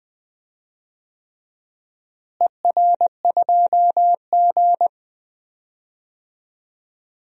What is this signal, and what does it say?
Morse code, heard as a keyed tone, about 21 wpm, 700 Hz. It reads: ER2G
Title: DXCC-100 Call Signs - 4 Characters - Repeated 3x - 20wpm